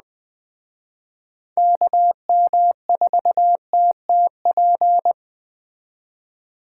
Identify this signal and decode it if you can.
Morse code, heard as a keyed tone, about 20 wpm, 700 Hz. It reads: KM4TTP
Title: Speed Racing - USA 2x3 Call Signs 17wpm